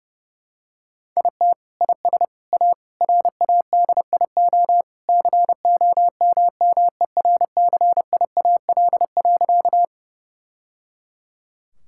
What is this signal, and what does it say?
Morse code, heard as a keyed tone, about 30 wpm, 700 Hz. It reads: IT IS A RADIO COMMERCIAL.
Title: Sentences from Top 1000 Words - Review Part 5 - 30wpm